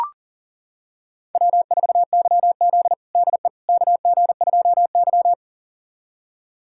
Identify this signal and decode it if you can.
Morse code, heard as a keyed tone, about 40 wpm, 700 Hz. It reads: W4YZ DE KG2Y
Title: USA Callsign DE Callsign - 4 Characters - 40wpm